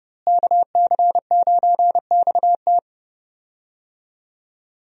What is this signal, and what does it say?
Morse code, heard as a keyed tone, about 30 wpm, 700 Hz. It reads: KC9XT
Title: USA Call Signs - 5 Characters - Repeated 2x - 30wpm